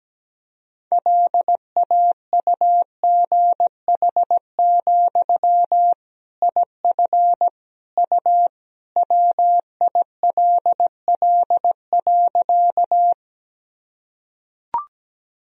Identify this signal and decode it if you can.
Morse code, heard as a keyed tone, about 17 wpm, 700 Hz. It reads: LAUGH, IF U WILL.
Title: Sentences from Top 500 Words and Abbreviations 17wpm